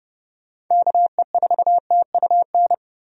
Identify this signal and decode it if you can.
Morse code, heard as a keyed tone, about 32 wpm, 700 Hz. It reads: KE4TUN
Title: USA Call Signs - 6 Characters - 30wpm